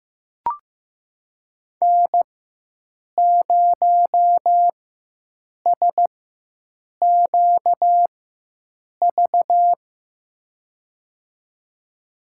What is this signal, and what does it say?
Morse code, heard as a keyed tone, about 15 wpm, spaced as for 7 wpm, 700 Hz. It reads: N0SQV
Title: USA Call Signs - 4x Character Spacing - 15wpm